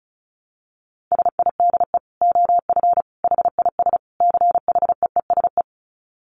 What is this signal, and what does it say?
Morse code, heard as a keyed tone, about 34 wpm, 700 Hz. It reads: SIDE OF HIS CHEESE
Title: Dit Challenge 35wpm